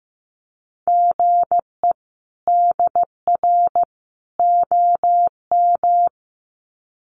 Morse code, heard as a keyed tone, about 15 wpm, 700 Hz. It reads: GE DR OM
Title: Common QSO Phrases 15wpm